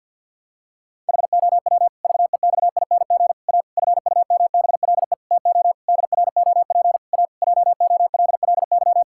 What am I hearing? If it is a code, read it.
Morse code, heard as a keyed tone, about 51 wpm, 700 Hz. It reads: HOW VEXING A FUMBLE TO DROP A JOLLY
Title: Pangram Challenge v2 - 50wpm